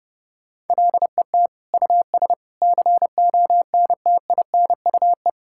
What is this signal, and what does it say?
Morse code, heard as a keyed tone, about 32 wpm, 700 Hz. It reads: LET US CONTINUE
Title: Sentences from Top 1000 Words - Review Part 6 - 30wpm